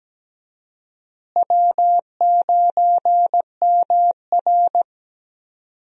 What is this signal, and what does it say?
Morse code, heard as a keyed tone, about 17 wpm, 700 Hz. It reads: W9MR
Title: USA Call Signs - 4 Characters - Repeated 3x - 17wpm